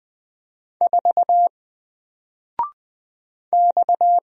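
Morse code, heard as a keyed tone, about 20 wpm, 700 Hz. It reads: 4 X
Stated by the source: Single letter and number Morse drill at 20 wpm, from N4LCD.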